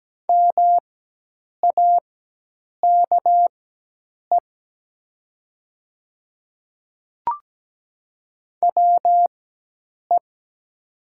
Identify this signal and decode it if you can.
Morse code, heard as a keyed tone, about 17 wpm, spaced as for 8 wpm, 700 Hz. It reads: MAKE WE
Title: 4x Character Spacing - Top 100 Words - 17wpm